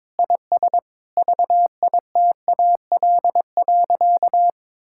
Morse code, heard as a keyed tone, about 22 wpm, 700 Hz. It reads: IS VITAL.
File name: practice / 22wpm / Sentences with 8-Word Phrases - 22wpm